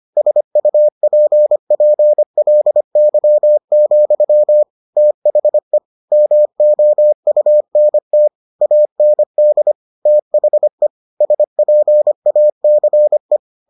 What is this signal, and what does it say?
Morse code, heard as a keyed tone, about 25 wpm, 600 Hz. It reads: SUPPLY, THE MOUNT AND THE SPACE